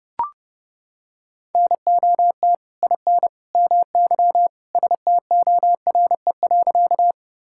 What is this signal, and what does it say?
Morse code, heard as a keyed tone, about 30 wpm, 700 Hz. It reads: NOT IN MY STORE.
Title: Sentences from Top 1000 Words - Review Part 5 - 30wpm